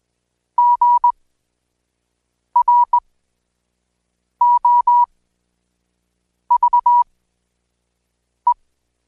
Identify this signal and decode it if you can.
Morse code, heard as a keyed tone, about 19 wpm, spaced as for 5 wpm, 970 Hz. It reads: GROVE
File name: Code Quick